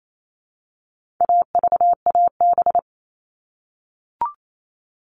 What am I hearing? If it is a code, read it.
Morse code, heard as a keyed tone, about 28 wpm, 700 Hz. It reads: AVAB
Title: All Cut Numbers - 4-Digit 28wpm